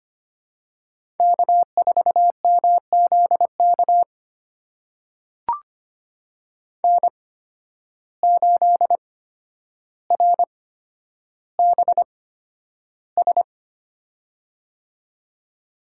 Morse code, heard as a keyed tone, about 25 wpm, spaced as for 11 wpm, 700 Hz. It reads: K4MZK N8RBS
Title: USA Call Signs - 5 Characters - 8x Character Spacing - 25wpm